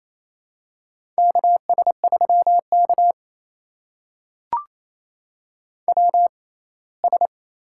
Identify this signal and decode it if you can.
Morse code, heard as a keyed tone, about 28 wpm, spaced as for 19 wpm, 700 Hz. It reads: KS3K WS
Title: USA Call Signs - 4 Characters - 6x Character Spacing - 28wpm